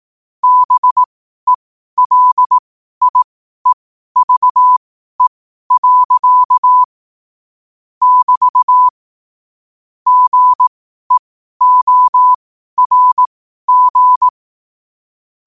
Morse code, heard as a keyed tone, about 18 wpm, spaced as for 12 wpm, 1000 Hz. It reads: BELIEVE. <BT> GEORG
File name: qotdfiles / 12wpm